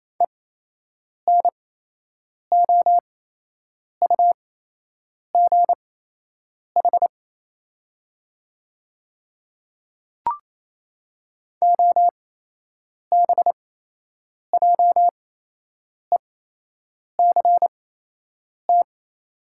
Morse code, heard as a keyed tone, about 28 wpm, spaced as for 8 wpm, 700 Hz. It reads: ENOUGH OBJECT
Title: 8x Character Spacing - Top 500 Words - 28wpm